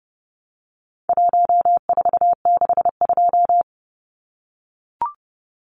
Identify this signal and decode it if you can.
Morse code, heard as a keyed tone, about 30 wpm, 700 Hz. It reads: 1462
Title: Numbers - 4-Digit 30wpm